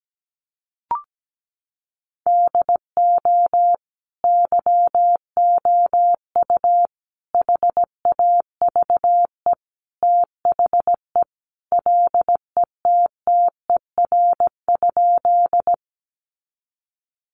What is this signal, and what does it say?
Morse code, heard as a keyed tone, about 17 wpm, 700 Hz. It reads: DO YOU HAVE THE LETTER?
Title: Sentences from Top 300 Words 17wpm